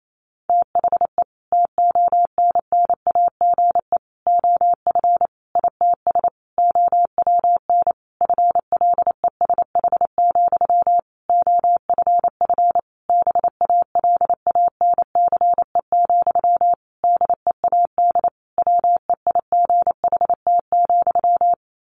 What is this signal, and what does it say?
Morse code, heard as a keyed tone, about 28 wpm, 700 Hz. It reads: THE TONNAGE OF ITS OWN FLESH, OFF BALANCE, DEAD WEIGHT,
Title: A Sound Of Thunder 28wpm